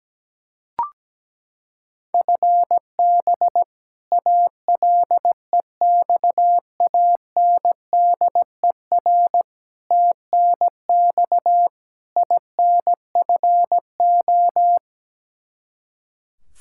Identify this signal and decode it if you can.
Morse code, heard as a keyed tone, about 17 wpm, 700 Hz. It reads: FB ALEXANDER TNX INFO
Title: K1USN SST QSO Practice 17wpm